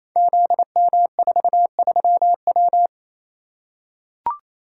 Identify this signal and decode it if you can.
Morse code, heard as a keyed tone, about 28 wpm, 700 Hz. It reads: ZM43W
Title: Code Groups - 5 Letter:Numbers - 28wpm